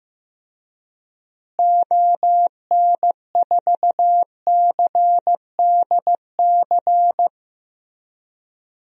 Morse code, heard as a keyed tone, about 15 wpm, 700 Hz. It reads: ON4CDC
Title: DXCC-100 Call Signs - 6 Characters - Repeated 3x - 15wpm